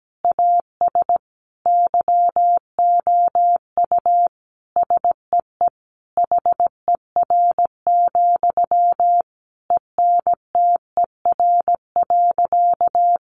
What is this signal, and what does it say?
Morse code, heard as a keyed tone, about 17 wpm, 700 Hz. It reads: AS YOU SEE HER, ENTER.
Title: Sentences from Top 1300 Words 17wpm